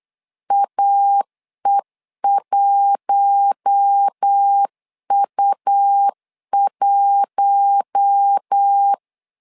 Morse code, heard as a keyed tone, about 8 wpm, 800 Hz. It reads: AE1U1
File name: LEZIONI / 4 Q1 8wpm 3035